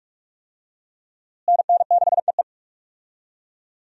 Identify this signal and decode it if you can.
Morse code, heard as a keyed tone, about 47 wpm, 700 Hz. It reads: NN6EE
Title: USA Call Signs - 5 Characters - Repeated 2x - 45wpm